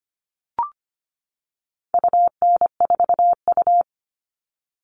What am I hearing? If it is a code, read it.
Morse code, heard as a keyed tone, about 25 wpm, 700 Hz. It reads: UN4U
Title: All Cut Numbers - 4-Digit 25wpm